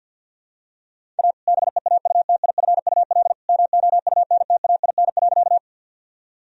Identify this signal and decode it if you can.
Morse code, heard as a keyed tone, about 52 wpm, 700 Hz. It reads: A BEAUTIFUL MOUNTAIN.
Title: Sentences from Top 1000 Words - Review Part 1 - 50wpm